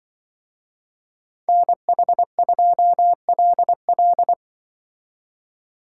Morse code, heard as a keyed tone, about 25 wpm, 700 Hz. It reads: NH2LL
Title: Speed Racing - USA 2x2 Call Signs 20wpm